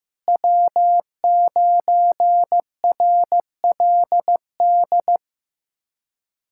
Morse code, heard as a keyed tone, about 15 wpm, 700 Hz. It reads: W9RLD
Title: USA Callsign DE Callsign 15wpm